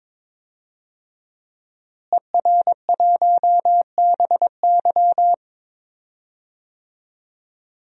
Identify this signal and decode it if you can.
Morse code, heard as a keyed tone, about 21 wpm, 700 Hz. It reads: ER1BY
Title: DXCC-100 Call Signs - 5 Characters - Repeated 2x - 22wpm